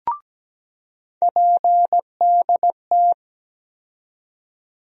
Morse code, heard as a keyed tone, about 17 wpm, 700 Hz. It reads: PDT
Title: USA Call Sign Postfixes - 3 Letters 17wpm